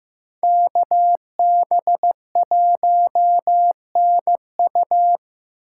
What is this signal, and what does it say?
Morse code, heard as a keyed tone, about 15 wpm, 700 Hz. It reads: KB1NU
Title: USA Call Signs - 2x Character Spacing - 15wpm